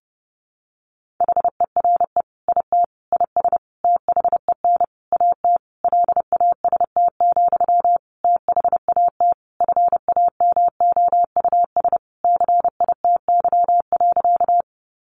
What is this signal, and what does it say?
Morse code, heard as a keyed tone, about 30 wpm, 700 Hz. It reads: HERE IT IS THEN AT LAST, THAT FAMOUS CITY.
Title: Sentences from Top 1500 Words 30wpm